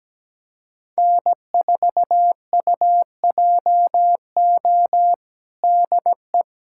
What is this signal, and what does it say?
Morse code, heard as a keyed tone, about 17 wpm, 700 Hz. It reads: N4UJO DE
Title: USA Callsign DE Callsign - 5 Characters - 17wpm